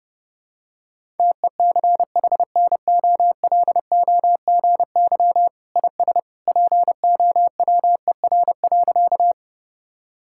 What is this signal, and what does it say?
Morse code, heard as a keyed tone, about 30 wpm, 700 Hz. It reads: TECHNOLOGY IS POWER.
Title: Sentences from Top 1000 Words - Review Part 5 - 30wpm